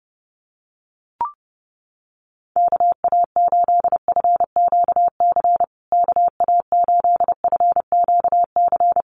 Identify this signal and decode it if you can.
Morse code, heard as a keyed tone, about 30 wpm, 700 Hz. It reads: KA8FQC KA8FQC
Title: USA Call Signs Repeated 3x 30wpm